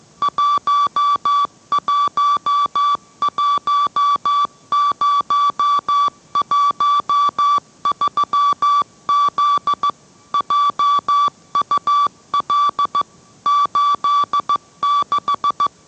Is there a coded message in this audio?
Yes: Morse code from 0.2 to 15.7 s, heard as a keyed tone, about 15 wpm, 1150 Hz. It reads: 111013Z JUL 86